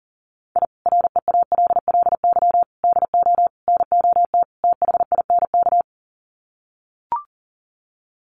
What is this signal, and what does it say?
Morse code, heard as a keyed tone, about 40 wpm, 700 Hz. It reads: I REALLY DO NOT THINK
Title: Sets of 5 Words Encore2 40wpm